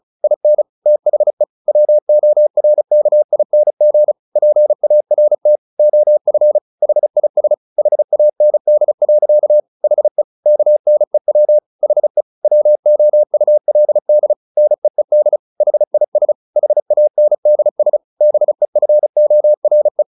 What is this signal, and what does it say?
Morse code, heard as a keyed tone, about 34 wpm, 600 Hz. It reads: IN THE WORKING PART OF HIS HAND. HE KNEW HE WOULD NEED HIS HANDS BEFORE
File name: the old man and the sea / the old man and the sea 1 35wpm